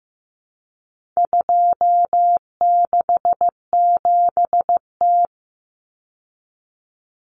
Morse code, heard as a keyed tone, about 15 wpm, 700 Hz. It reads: 267T
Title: Common Cut Numbers - 4-Digit 15wpm